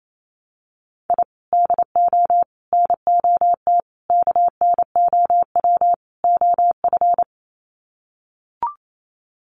Morse code, heard as a keyed tone, about 28 wpm, 700 Hz. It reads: I DO NOT KNOW OF
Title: Sets of 5 Words Encore2 28wpm